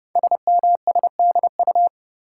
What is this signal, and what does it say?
Morse code, heard as a keyed tone, about 30 wpm, 700 Hz. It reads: SMSDU
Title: Code Groups - 5 Letters - 30wpm